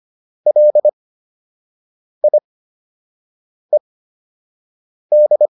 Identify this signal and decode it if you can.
Morse code, heard as a keyed tone, about 25 wpm, spaced as for 6 wpm, 600 Hz. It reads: LIED